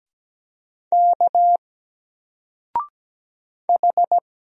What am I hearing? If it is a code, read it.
Morse code, heard as a keyed tone, about 17 wpm, 700 Hz. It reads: K H
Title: Single Letter-Number 17wpm